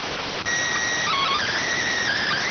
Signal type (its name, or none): SSTV (slow-scan television)